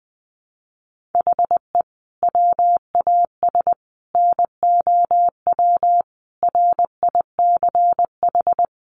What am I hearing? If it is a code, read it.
Morse code, heard as a keyed tone, about 20 wpm, 700 Hz. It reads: HE WAS NOW RICH
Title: Sentences from Top 1500 Words 20wpm